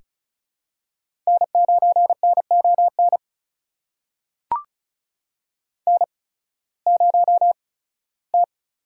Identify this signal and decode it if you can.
Morse code, heard as a keyed tone, about 36 wpm, spaced as for 19 wpm, 700 Hz. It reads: N9NON N0T
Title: USA Call Signs - 5 Characters - 8x Character Spacing - 35wpm